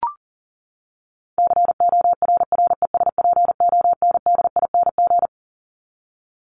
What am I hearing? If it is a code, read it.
Morse code, heard as a keyed tone, about 40 wpm, 700 Hz. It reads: CORRESPONDING
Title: ING Suffix - Top 100 Words with Base Words 40wpm